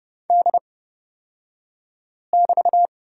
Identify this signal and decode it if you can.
Morse code, heard as a keyed tone, about 29 wpm, spaced as for 5 wpm, 700 Hz. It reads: D<BT>
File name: Lesson 36d - BT - Two Characters 30wpm